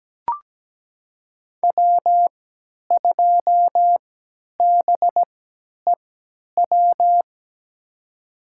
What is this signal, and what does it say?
Morse code, heard as a keyed tone, about 17 wpm, spaced as for 10 wpm, 700 Hz. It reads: W2BEW
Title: USA Call Signs - 5 Characters - 3x Character Spacing - 17wpm